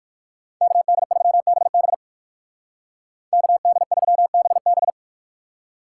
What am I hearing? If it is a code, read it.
Morse code, heard as a keyed tone, about 54 wpm, 700 Hz. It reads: KD3BB KD3BB
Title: Speed Racing - USA 2x2 Call Signs 45wpm